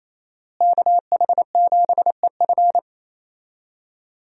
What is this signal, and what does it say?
Morse code, heard as a keyed tone, about 28 wpm, 700 Hz. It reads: KH7EF
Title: USA Call Signs - 5 Characters - Repeated 2x - 28wpm